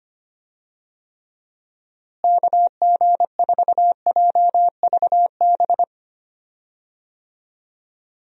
Morse code, heard as a keyed tone, about 25 wpm, 700 Hz. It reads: KG4JVB
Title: USA Call Signs - 6 Characters - Repeated 2x - 25wpm